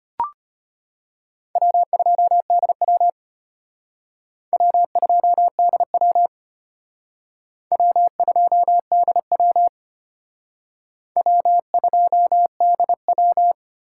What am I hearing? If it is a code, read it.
Morse code, heard as a keyed tone, about 33 wpm, 700 Hz. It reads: W2DW W2DW W2DW W2DW